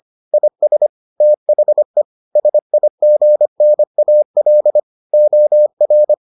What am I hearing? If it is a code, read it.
Morse code, heard as a keyed tone, about 25 wpm, 600 Hz. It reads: IS THE SIGNAL OR